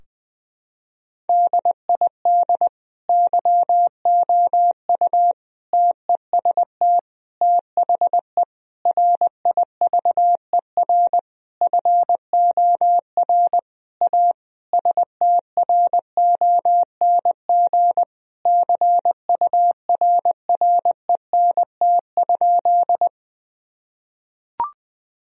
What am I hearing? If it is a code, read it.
Morse code, heard as a keyed tone, about 21 wpm, 700 Hz. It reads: DID YOU TEST THE RIVER FOR A STRONG CURRENT?
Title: Sentences from Top 1000 Words - Review Part 4 - 20wpm